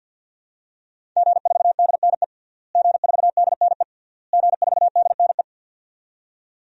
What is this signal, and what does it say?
Morse code, heard as a keyed tone, about 51 wpm, 700 Hz. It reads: G4DNE G4DNE G4DNE
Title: DXCC-100 Call Signs - 5 Characters - Repeated 3x - 50wpm